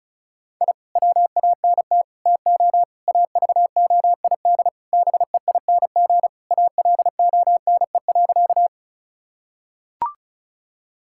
Morse code, heard as a keyed tone, about 36 wpm, 700 Hz. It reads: I WANT TO AVOID BEING ALONE.
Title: Sentences from Top 1000 Words - Review Part 6 - 35wpm